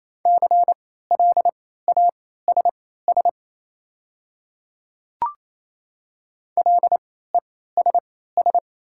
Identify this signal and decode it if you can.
Morse code, heard as a keyed tone, about 28 wpm, spaced as for 16 wpm, 700 Hz. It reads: CLASS LESS
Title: 3x Character Spacing - Top 500 Words - 28wpm